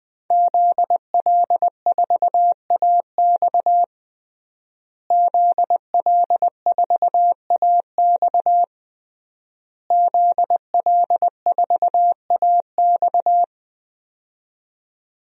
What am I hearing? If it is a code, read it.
Morse code, heard as a keyed tone, about 20 wpm, 700 Hz. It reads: ZL4AX ZL4AX ZL4AX
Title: DXCC-100 Call Signs - 5 Characters - Repeated 3x - 20wpm